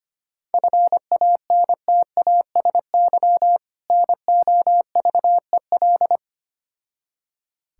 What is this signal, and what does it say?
Morse code, heard as a keyed tone, about 25 wpm, 700 Hz. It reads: FANTASY NOVEL
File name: practice / 25wpm / Difficult Words with Hints - 25wpm